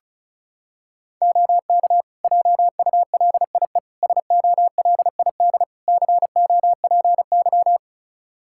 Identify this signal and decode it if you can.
Morse code, heard as a keyed tone, about 34 wpm, 700 Hz. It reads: OK JULIE SOLID COPY
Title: Practice QSO - Exchange 3 - More Info - 35wpm